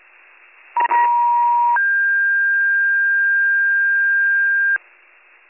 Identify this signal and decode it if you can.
Two-tone page, heard as a one-second tone, then a three-second tone, about 950 then 1600 Hz, with noise. It is heard from 0.8 to 4.8 s.